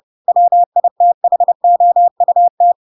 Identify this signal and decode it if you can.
Morse code, heard as a keyed tone, about 30 wpm, 700 Hz. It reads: WITHOUT
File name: Pinocchio / Pinocchio ch3 30 30 ik1hge